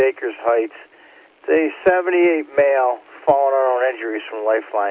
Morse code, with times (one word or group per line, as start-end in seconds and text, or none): none